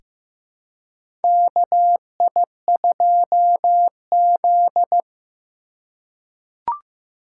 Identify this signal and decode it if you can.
Morse code, heard as a keyed tone, about 15 wpm, 700 Hz. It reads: KI2Z
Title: USA Call Signs - 4 Characters - 2x Character Spacing - 15wpm